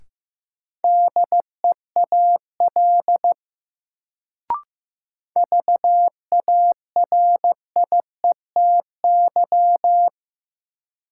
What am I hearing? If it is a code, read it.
Morse code, heard as a keyed tone, about 15 wpm, 700 Hz. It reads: DEAL VARIETY
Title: Sets of 3 Words - New Word Drill - 15wpm